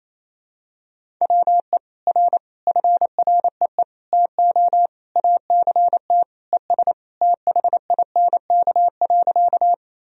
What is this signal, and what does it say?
Morse code, heard as a keyed tone, about 28 wpm, 700 Hz. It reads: WE R FREE TO ACT ES THINK.
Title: Sentences from Top 500 Words and Abbreviations 28wpm